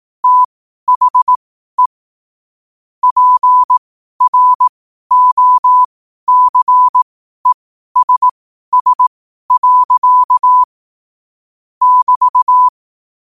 Morse code, heard as a keyed tone, about 18 wpm, spaced as for 12 wpm, 1000 Hz. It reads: THE PROCESS. <BT>
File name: qotdfiles / 12wpm